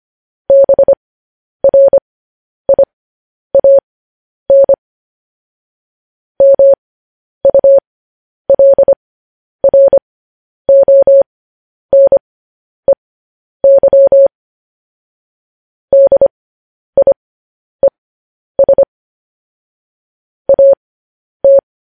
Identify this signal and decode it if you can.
Morse code, heard as a keyed tone, about 25 wpm, spaced as for 10 wpm, 550 Hz. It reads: BRIAN MULRONEY DIES AT